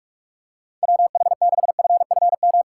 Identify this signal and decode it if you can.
Morse code, heard as a keyed tone, about 46 wpm, 700 Hz. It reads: WH6FFM
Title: USA Call Signs - 6 Characters - 2x Character Spacing - 45wpm